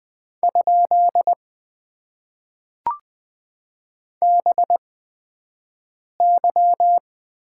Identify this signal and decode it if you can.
Morse code, heard as a keyed tone, about 20 wpm, spaced as for 5 wpm, 700 Hz. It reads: ? BY
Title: Top 90 QSO Elements - 8x Character Spacing - 20wpm